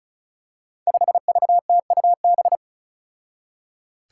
Morse code, heard as a keyed tone, about 34 wpm, 700 Hz. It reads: 5VTUB
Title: Five Challenging Letter-Number Combinations 35wpm